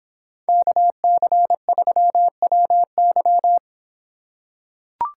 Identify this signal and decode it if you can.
Morse code, heard as a keyed tone, about 25 wpm, 700 Hz. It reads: KC3WY